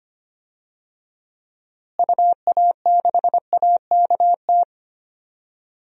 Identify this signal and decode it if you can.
Morse code, heard as a keyed tone, about 25 wpm, 690 Hz. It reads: UA6AKT
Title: DXCC-100 Call Signs - 6 Characters - Repeated 3x - 25wpm